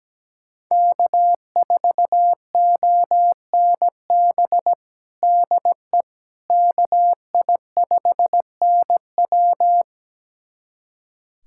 Morse code, heard as a keyed tone, about 17 wpm, 700 Hz. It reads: K4ONB DE KI5NW
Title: USA Callsign DE Callsign 17wpm